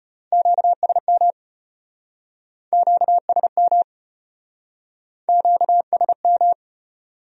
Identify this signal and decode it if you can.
Morse code, heard as a keyed tone, about 37 wpm, 700 Hz. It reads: QSM QSM QSM